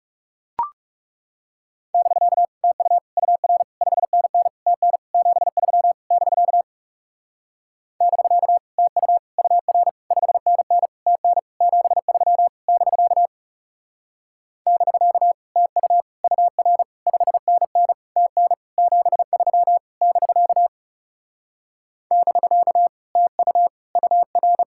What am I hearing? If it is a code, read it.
Morse code, heard as a keyed tone, about 38 wpm, 700 Hz. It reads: <BK> TU UR 5NN TN 73 <BK> <BK> TU UR 5NN TN 73 <BK> <BK> TU UR 5NN TN 73 <BK> <BK> TU UR